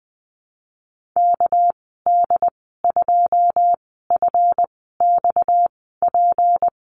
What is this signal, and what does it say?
Morse code, heard as a keyed tone, about 20 wpm, spaced as for 15 wpm, 700 Hz. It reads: KD2FXP
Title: 231 Lesson 37g - Letter x - Call Signs 20wpm